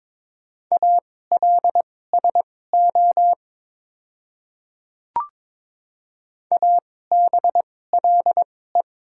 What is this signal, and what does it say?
Morse code, heard as a keyed tone, about 23 wpm, spaced as for 16 wpm, 690 Hz. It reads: ALSO ABLE
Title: 2x Character Spacing - Top 500 Words - 22wpm